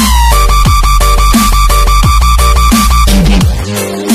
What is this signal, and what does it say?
There is no code.